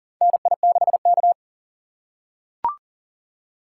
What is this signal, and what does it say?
Morse code, heard as a keyed tone, about 40 wpm, 700 Hz. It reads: NI6K
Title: USA Call Signs - 4 Characters - 3x Character Spacing - 40wpm